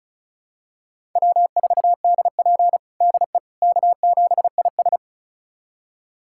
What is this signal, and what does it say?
Morse code, heard as a keyed tone, about 36 wpm, 700 Hz. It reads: W4DP DE K7IS
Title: USA Callsign DE Callsign - 4 Characters - 35wpm